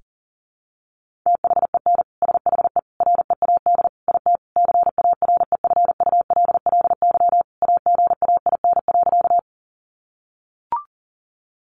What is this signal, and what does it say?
Morse code, heard as a keyed tone, about 40 wpm, 700 Hz. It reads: THEN SHE READ IT CAREFULLY AGAIN.
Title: Sentences from Top 1300 Words 40wpm